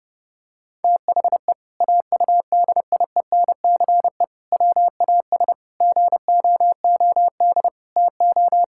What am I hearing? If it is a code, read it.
Morse code, heard as a keyed tone, about 32 wpm, 700 Hz. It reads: THE AUDIENCE WAS GOOD TO
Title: Sentences from Top 1000 Words - Review Part 5 - 30wpm